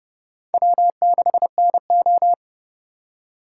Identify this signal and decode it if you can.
Morse code, heard as a keyed tone, about 31 wpm, 700 Hz. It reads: W6NO
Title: USA Call Signs - 4 Characters - 3x Character Spacing - 30wpm